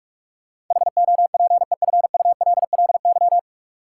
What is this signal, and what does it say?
Morse code, heard as a keyed tone, about 46 wpm, 700 Hz. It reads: HOPEFULLY
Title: Sentences from Top 1500 Words - New Word Drill - 45wpm